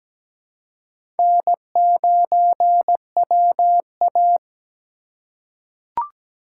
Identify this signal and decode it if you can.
Morse code, heard as a keyed tone, about 17 wpm, 700 Hz. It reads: N9WA
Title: USA Call Signs - 4 Characters - 2x Character Spacing - 17wpm